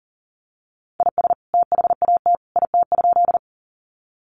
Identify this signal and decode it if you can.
Morse code, heard as a keyed tone, about 40 wpm, 700 Hz. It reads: IS THAT IT?
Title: Sentences from Top 100 Words 40wpm